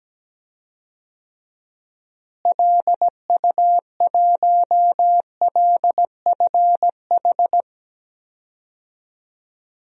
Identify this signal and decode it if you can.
Morse code, heard as a keyed tone, about 17 wpm, 700 Hz. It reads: LU1LFH
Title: DXCC-100 Call Signs - 6 Characters - Repeated 3x - 17wpm